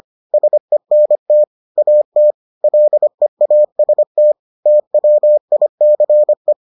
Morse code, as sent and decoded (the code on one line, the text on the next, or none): ... -. - / .- - / .-.. .- ... - / - .-- .. -.-. .
SENT AT LEAST TWICE